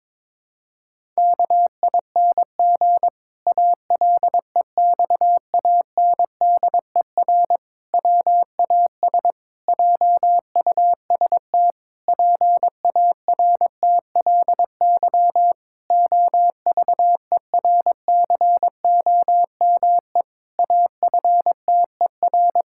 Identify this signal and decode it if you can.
Morse code, heard as a keyed tone, about 22 wpm, 700 Hz. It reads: KING ALEXANDER WAS JUST PARTLY OVERCOME AFTER
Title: Pangram Challenge v2 - 22wpm